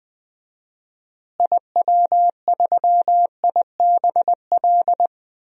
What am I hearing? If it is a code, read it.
Morse code, heard as a keyed tone, about 21 wpm, 700 Hz. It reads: IW3IBL
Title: DXCC-100 Call Signs - 6 Characters - Repeated 3x - 20wpm